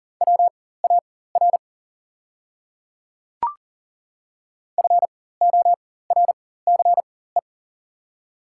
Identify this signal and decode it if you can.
Morse code, heard as a keyed tone, about 40 wpm, spaced as for 19 wpm, 700 Hz. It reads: WAR FORCE
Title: 4x Character Spacing - Top 500 Words - 40wpm